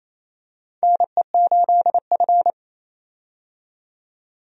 Morse code, heard as a keyed tone, about 27 wpm, 700 Hz. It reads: NE8F
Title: USA Call Signs - 4 Characters - 28wpm